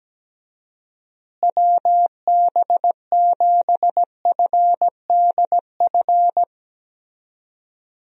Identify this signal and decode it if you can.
Morse code, heard as a keyed tone, about 17 wpm, 700 Hz. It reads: WB7FDF
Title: USA Call Signs - 6 Characters - Repeated 2x - 17wpm